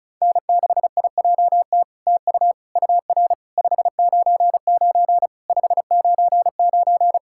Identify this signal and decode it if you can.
Morse code, heard as a keyed tone, about 36 wpm, 700 Hz. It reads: N6IJT TU UR 599 599
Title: SOTA QSO Practice 35wpm